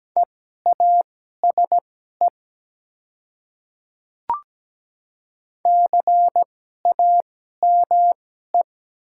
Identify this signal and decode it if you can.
Morse code, heard as a keyed tone, about 17 wpm, spaced as for 12 wpm, 700 Hz. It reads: EASE CAME
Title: 2x Character Spacing - Top 300 Words - 17wpm